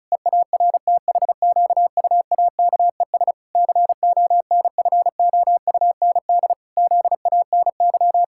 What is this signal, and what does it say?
Morse code, heard as a keyed tone, about 36 wpm, 700 Hz. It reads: EARTHQUAKES CONFOUND ZANY
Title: Pangram Challenge v2 - 35wpm